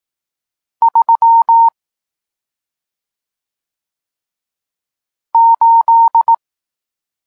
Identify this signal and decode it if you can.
Morse code, heard as a keyed tone, about 18 wpm, 920 Hz. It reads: 3 8
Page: Tekst 40 – Nauka znaków przestankowych